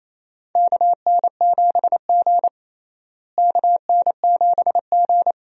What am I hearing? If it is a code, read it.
Morse code, heard as a keyed tone, about 28 wpm, 700 Hz. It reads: KN7G KN7G
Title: USA Call Signs - 4 Characters - Repeated 2x - 28wpm